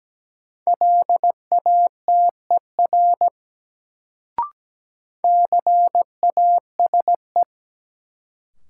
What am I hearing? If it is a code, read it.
Morse code, heard as a keyed tone, about 17 wpm, 700 Hz. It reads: LATER CASE
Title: Sentences from Top 700 Words - New Word Drill - 17wpm